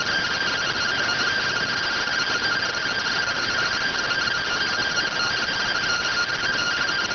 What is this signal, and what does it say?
Two-tone teleprinter signal, about 170 Hz shift.